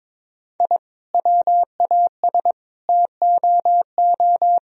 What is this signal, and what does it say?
Morse code, heard as a keyed tone, about 22 wpm, 700 Hz. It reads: I WAS TOO
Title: Sentences from Top 1000 Words - Review Part 4 - 22wpm